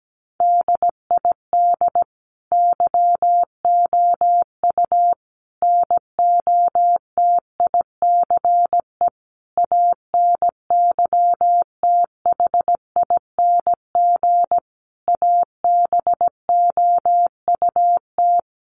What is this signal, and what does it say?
Morse code, heard as a keyed tone, about 17 wpm, 700 Hz. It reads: DID YOU NOTICE ANYTHING ABOUT
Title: Sentences from Top 1000 Words 17wpm